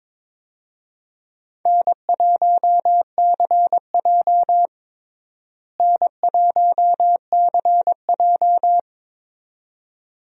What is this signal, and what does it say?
Morse code, heard as a keyed tone, about 22 wpm, 700 Hz. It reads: N1CJ N1CJ
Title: USA Call Signs - 4 Characters - Repeated 2x - 22wpm